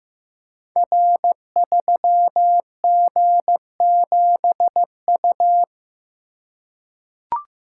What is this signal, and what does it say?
Morse code, heard as a keyed tone, about 15 wpm, 700 Hz. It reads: R3G7U
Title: Code Groups - 5 Letter:Numbers - 15wpm